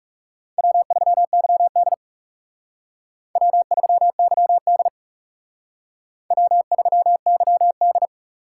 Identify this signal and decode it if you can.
Morse code, heard as a keyed tone, about 37 wpm, 690 Hz. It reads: W3YD W3YD W3YD
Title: Speed Racing - USA Call Signs 30wpm